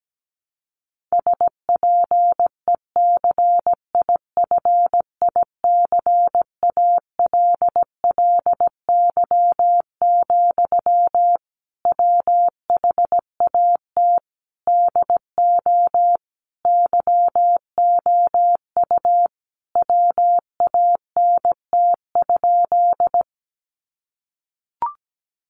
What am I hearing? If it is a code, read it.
Morse code, heard as a keyed tone, about 17 wpm, 700 Hz. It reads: SPECIFICALLY, WHAT DO YOU WANT?
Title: Sentences from Top 1100 Words 17wpm